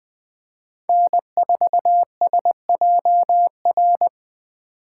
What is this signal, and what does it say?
Morse code, heard as a keyed tone, about 20 wpm, 700 Hz. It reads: N4SJR
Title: Speed Racing - USA Call Signs 17wpm